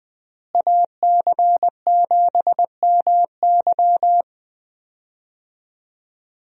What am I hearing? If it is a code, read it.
Morse code, heard as a keyed tone, about 20 wpm, 700 Hz. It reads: AC7MY
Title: USA Call Signs - 5 Characters - Repeated 2x - 20wpm